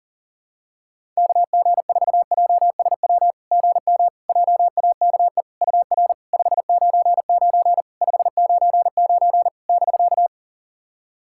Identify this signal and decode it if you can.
Morse code, heard as a keyed tone, about 40 wpm, 700 Hz. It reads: KG4JSW GM JAKE UR 599 599 <BK>
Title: SOTA QSO Practice 40wpm